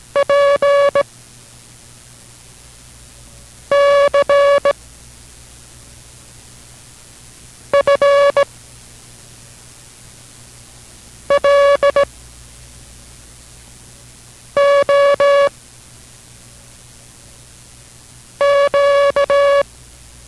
Morse code, heard as a keyed tone, about 18 wpm, 560 Hz. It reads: P C F L O Q